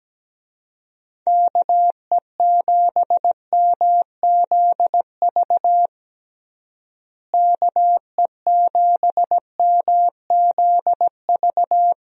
Morse code, heard as a keyed tone, about 17 wpm, 700 Hz. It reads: KE7MZV KE7MZV
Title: USA Call Signs - 6 Characters - Repeated 2x - 17wpm